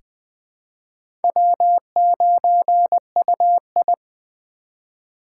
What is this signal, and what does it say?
Morse code, heard as a keyed tone, about 21 wpm, 700 Hz. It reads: W9UI
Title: USA Call Signs - 4 Characters - 2x Character Spacing - 20wpm